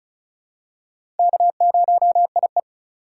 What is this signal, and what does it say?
Morse code, heard as a keyed tone, about 34 wpm, 700 Hz. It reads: K0IE
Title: USA Call Signs - 4 Characters - 4x Character Spacing - 35wpm